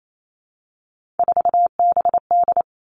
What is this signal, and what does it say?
Morse code, heard as a keyed tone, about 28 wpm, 700 Hz. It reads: 4BD
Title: Three Challenging Letter-Number Combinations 28wpm